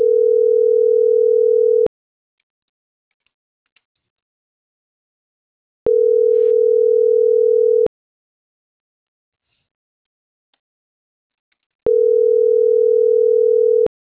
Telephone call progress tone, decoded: ringback tone

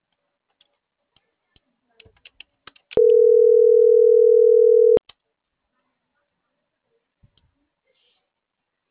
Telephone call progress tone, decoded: ringback tone

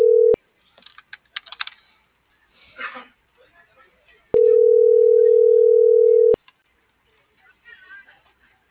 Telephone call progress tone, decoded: ringback tone

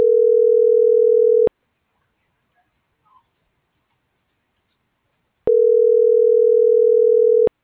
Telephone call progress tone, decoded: ringback tone

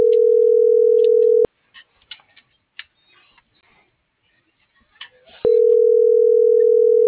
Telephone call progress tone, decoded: ringback tone

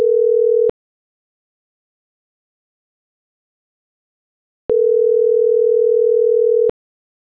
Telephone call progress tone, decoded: ringback tone

